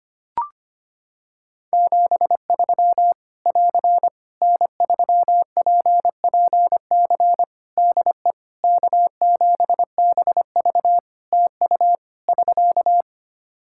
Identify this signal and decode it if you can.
Morse code, heard as a keyed tone, about 25 wpm, 700 Hz. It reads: 73 <AR> N3PPC DE K7BV TU <SK>
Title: Practice QSO - Exchange 4 - Ending - 25wpm